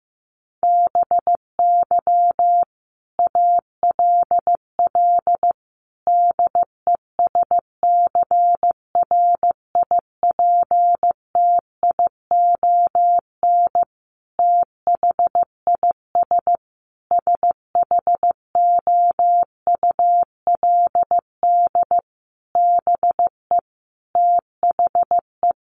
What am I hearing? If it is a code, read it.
Morse code, heard as a keyed tone, about 15 wpm, 700 Hz. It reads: BY ALL DESCRIPTION THIS SHOULD BE THE
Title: Sentences from Top 1300 Words 15wpm